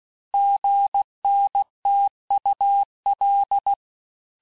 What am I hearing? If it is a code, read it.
Morse code, heard as a keyed tone, about 16 wpm, 800 Hz. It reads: GNTUL
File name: cours-du-lundi files / cours-1592--teste-chiffre-vitesse-15mpm